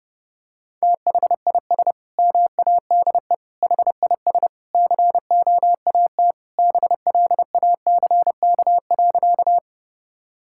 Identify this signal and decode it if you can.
Morse code, heard as a keyed tone, about 30 wpm, 700 Hz. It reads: THIS MADE HIS COAT BLACK.
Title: Sentences from Top 1600 Words 30wpm